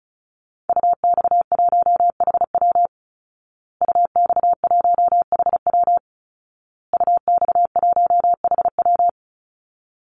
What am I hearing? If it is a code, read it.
Morse code, heard as a keyed tone, about 34 wpm, 700 Hz. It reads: UX1HW UX1HW UX1HW
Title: Call Signs Repeated 2x - 35wpm